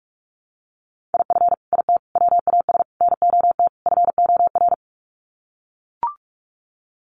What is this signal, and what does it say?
Morse code, heard as a keyed tone, about 44 wpm, 700 Hz. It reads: IF IT WAS NOT FOR
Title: Sets of 5 Words Encore 45wpm